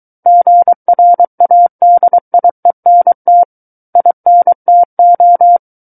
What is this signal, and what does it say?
Morse code, heard as a keyed tone, about 23 wpm, 700 Hz. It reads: GRADIENT INTO
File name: Book One / 14-Chapter 14 - 23 WPM